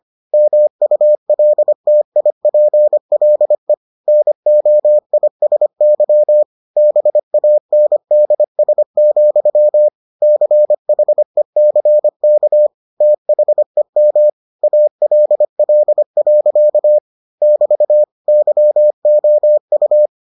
Morse code, heard as a keyed tone, about 25 wpm, 600 Hz. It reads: MULTIPLE NOISY BANDS, CHECK THEM ALL. <BT> YOU